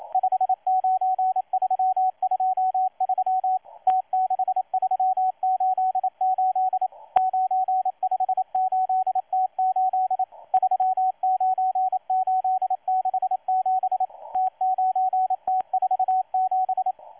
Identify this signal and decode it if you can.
Morse code, heard as a keyed tone, about 27 wpm, 750 Hz, with radio noise. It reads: 59323 T6388 958T8 39867 T9T47
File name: M12-2016-07-07-1310utc-13926khz-error-restart